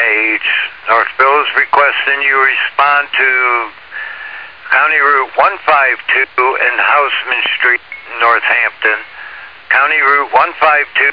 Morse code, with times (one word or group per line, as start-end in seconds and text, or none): none